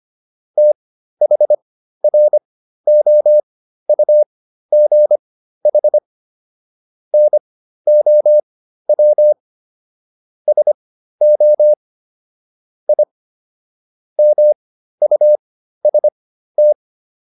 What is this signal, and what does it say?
Morse code, heard as a keyed tone, about 25 wpm, spaced as for 13 wpm, 600 Hz. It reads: THROUGH NOW SO I MUST